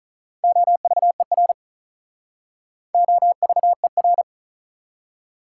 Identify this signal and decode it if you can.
Morse code, heard as a keyed tone, about 43 wpm, 700 Hz. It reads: OVER OVER
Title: Speed-Racing Prefixes and Suffixes 35wpm